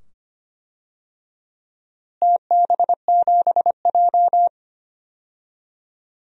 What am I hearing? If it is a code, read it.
Morse code, heard as a keyed tone, about 25 wpm, 700 Hz. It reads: TB7J
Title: DXCC-100 Call Signs - 4 Characters - Repeated 2x - 25wpm